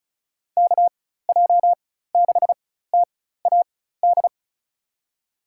Morse code, heard as a keyed tone, about 36 wpm, spaced as for 16 wpm, 700 Hz. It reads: KJ6TAD
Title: USA Call Signs - 6 Characters - 4x Character Spacing - 35wpm